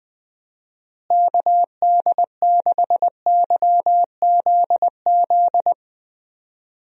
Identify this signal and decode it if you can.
Morse code, heard as a keyed tone, about 20 wpm, 700 Hz. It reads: KD6YZZ
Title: USA Call Signs - 6 Characters - Repeated 3x - 20wpm